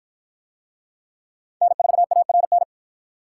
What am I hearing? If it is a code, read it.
Morse code, heard as a keyed tone, about 53 wpm, 700 Hz. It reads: N4ARN